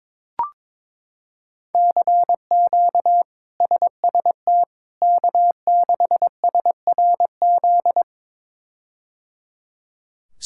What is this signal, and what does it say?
Morse code, heard as a keyed tone, about 22 wpm, 700 Hz. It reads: CQ SST K6SRZ
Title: K1USN SST QSO Practice - 22wpm